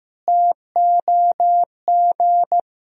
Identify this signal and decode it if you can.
Morse code, heard as a keyed tone, about 15 wpm, 700 Hz. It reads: TOG
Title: Sentences from Top 1000 Words - Review Part 1 - 15wpm